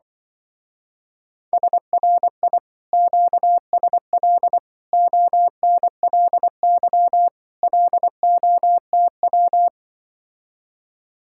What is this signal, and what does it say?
Morse code, heard as a keyed tone, about 24 wpm, 700 Hz. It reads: SRI QSL ONLY LOTW
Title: Speed-Racing Common QSO Phrases Encore - 20wpm